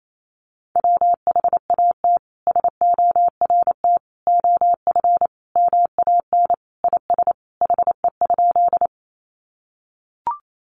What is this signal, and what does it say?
Morse code, heard as a keyed tone, about 28 wpm, 700 Hz. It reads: WHAT SORT OF MAN IS HE?
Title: Sentences from Top 1300 Words 28wpm